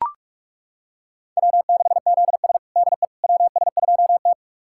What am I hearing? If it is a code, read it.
Morse code, heard as a keyed tone, about 42 wpm, 700 Hz. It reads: W6ZS DE WS2T
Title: USA Callsign DE Callsign - 4 Characters - 45wpm